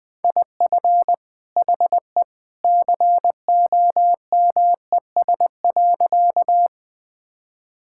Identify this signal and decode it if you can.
Morse code, heard as a keyed tone, about 21 wpm, 700 Hz. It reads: IF HE COMES.
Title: Sentences with 2-Word Phrases - 20wpm